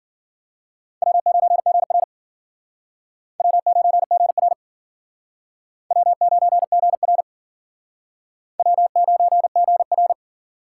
Morse code, heard as a keyed tone, about 54 wpm, 700 Hz. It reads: W9GR W9GR W9GR W9GR